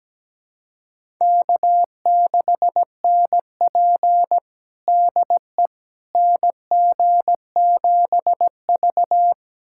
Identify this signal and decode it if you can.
Morse code, heard as a keyed tone, about 17 wpm, 700 Hz. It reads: K6NP DE NG7V
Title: USA Callsign DE Callsign - 4 Characters - 17wpm